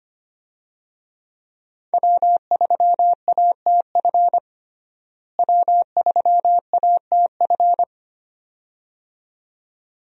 Morse code, heard as a keyed tone, about 25 wpm, 700 Hz. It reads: W3ATF W3ATF
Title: USA Call Signs - 5 Characters - Repeated 2x - 25wpm